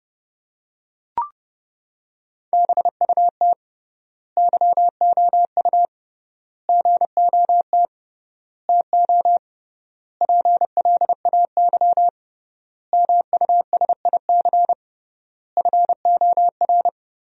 Morse code, heard as a keyed tone, about 30 wpm, 700 Hz. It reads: BUT YOU GOT TO PLAY MUSIC FOR
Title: 3x Word-Spacing - Sentences from Top 300 Words - 30wpm